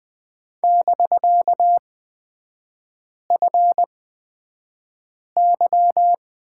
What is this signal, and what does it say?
Morse code, heard as a keyed tone, about 20 wpm, spaced as for 5 wpm, 700 Hz. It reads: <BK>FY
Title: 250 Lesson 40e - BK - Three Characters 20wpm